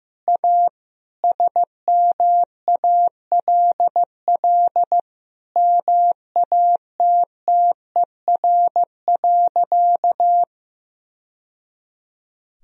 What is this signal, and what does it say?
Morse code, heard as a keyed tone, about 15 wpm, 700 Hz. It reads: A SMALL MATTER.
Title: Sentences from Top 1000 Words - Review Part 1 - 15wpm